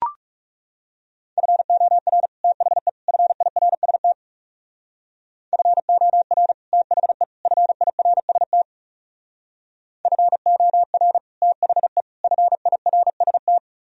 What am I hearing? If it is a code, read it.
Morse code, heard as a keyed tone, about 37 wpm, 700 Hz. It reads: FOR THE FIRST FOR THE FIRST FOR THE FIRST
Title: Speed-Racing - Sets of 3 Words 30wpm